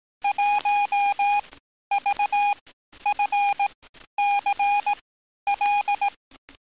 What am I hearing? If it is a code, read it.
Morse code, heard as a keyed tone, about 18 wpm, spaced as for 11 wpm, 820 Hz. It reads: 1VFCL
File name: training / NAVPERS 11049to11060 / NAVPERS 11050 RA RB